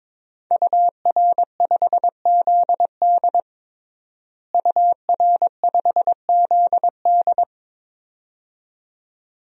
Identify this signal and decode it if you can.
Morse code, heard as a keyed tone, about 22 wpm, 700 Hz. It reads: UR5ZD UR5ZD
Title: DXCC-100 Call Signs - 5 Characters - Repeated 2x - 22wpm